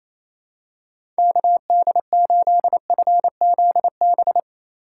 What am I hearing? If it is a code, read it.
Morse code, heard as a keyed tone, about 28 wpm, 700 Hz. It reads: KD8FZB
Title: USA Call Signs - 6 Characters - 2x Character Spacing - 28wpm